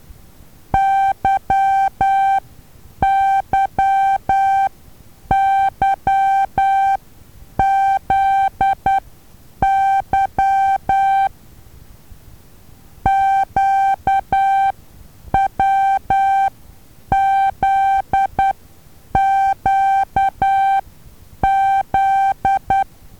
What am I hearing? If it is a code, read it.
Morse code, heard as a keyed tone, about 12 wpm, spaced as for 8 wpm, 780 Hz. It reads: YYYZY QWZQZ
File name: Taviro HA6VA B